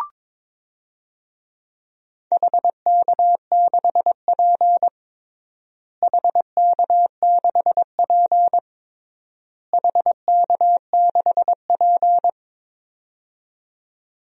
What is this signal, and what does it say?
Morse code, heard as a keyed tone, about 22 wpm, 700 Hz. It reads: HK6P HK6P HK6P
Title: DXCC-100 Call Signs - 4 Characters - Repeated 3x - 22wpm